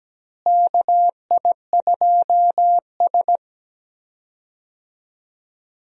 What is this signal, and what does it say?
Morse code, heard as a keyed tone, about 17 wpm, 700 Hz. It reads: KI2S
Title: USA Call Signs - 4 Characters - Repeated 2x - 17wpm